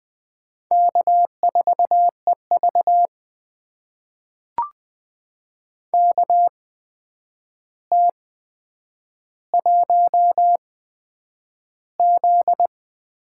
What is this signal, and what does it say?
Morse code, heard as a keyed tone, about 20 wpm, spaced as for 9 wpm, 700 Hz. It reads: K4EV KT1Z
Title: USA Call Signs - 4 Characters - 8x Character Spacing - 20wpm